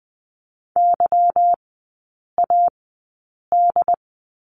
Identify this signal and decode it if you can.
Morse code, heard as a keyed tone, about 20 wpm, spaced as for 8 wpm, 700 Hz. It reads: YAD
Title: 145 Lesson 24e - Letter y - Three Characters 20wpm